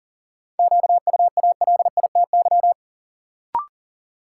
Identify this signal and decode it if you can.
Morse code, heard as a keyed tone, about 39 wpm, 700 Hz. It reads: QUALITY
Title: Sentences from Top 1000 Words - Review Part 4 - New Word Drill - 40wpm